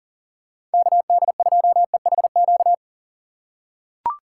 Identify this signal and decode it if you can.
Morse code, heard as a keyed tone, about 40 wpm, 700 Hz. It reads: KD2EHQ
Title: USA Call Signs - 6 Characters - 2x Character Spacing - 40wpm